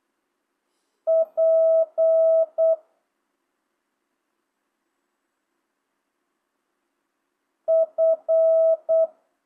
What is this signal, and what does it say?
Morse code, heard as a keyed tone, about 8 wpm, 650 Hz. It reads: P F